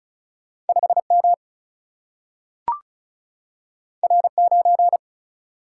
Morse code, heard as a keyed tone, about 36 wpm, 700 Hz. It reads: 5M R9
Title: Single-Letter Single-Number 35wpm